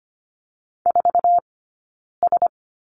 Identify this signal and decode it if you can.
Morse code, heard as a keyed tone, about 25 wpm, 700 Hz. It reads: 4 S
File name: Lesson 15d - Number 5 - Two Characters 25wpm